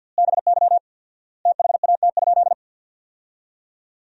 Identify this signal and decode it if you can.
Morse code, heard as a keyed tone, about 52 wpm, 700 Hz. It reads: BY THAT?
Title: 4x Word-Spacing - Sentences from Top 200 Words - 50wpm